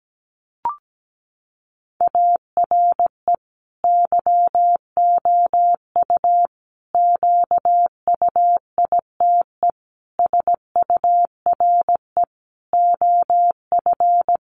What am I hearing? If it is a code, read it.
Morse code, heard as a keyed tone, about 17 wpm, 700 Hz. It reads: ARE YOU QUITE SURE OF
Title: Sentences from Top 700 Words 17wpm